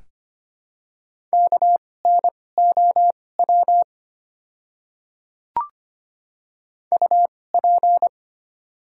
Morse code, heard as a keyed tone, about 25 wpm, spaced as for 18 wpm, 700 Hz. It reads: KNOW UP
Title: 2x Character Spacing - Top 200 Words - 25wpm